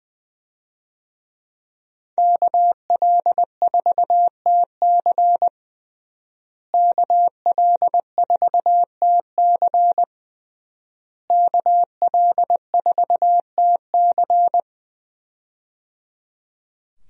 Morse code, heard as a keyed tone, about 20 wpm, 700 Hz. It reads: KL4TC KL4TC KL4TC
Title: USA Call Signs - 5 Characters - Repeated 3x - 20wpm